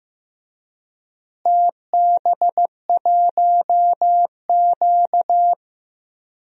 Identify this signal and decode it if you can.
Morse code, heard as a keyed tone, about 15 wpm, 700 Hz. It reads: TB1Q
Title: DXCC-100 Call Signs - 4 Characters - Repeated 3x - 15wpm